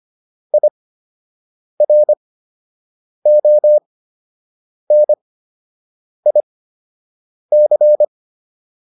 Morse code, heard as a keyed tone, about 25 wpm, spaced as for 7 wpm, 600 Hz. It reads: IRONIC
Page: fundamental-practice-files